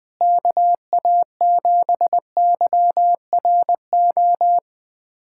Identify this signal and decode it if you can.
Morse code, heard as a keyed tone, about 21 wpm, 700 Hz. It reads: KA7YRO
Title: USA Callsign DE Callsign - 6 Characters - 20wpm